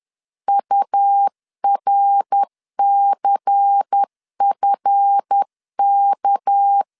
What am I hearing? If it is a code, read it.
Morse code, heard as a keyed tone, about 10 wpm, 800 Hz. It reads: URCFK
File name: LEZIONI / 7 GF9Z / 7 GF 10wpm 2514